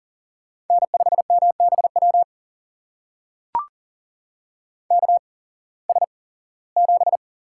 Morse code, heard as a keyed tone, about 40 wpm, spaced as for 21 wpm, 700 Hz. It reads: N5MBW KS7
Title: USA Call Signs - 8x Character Spacing - 40wpm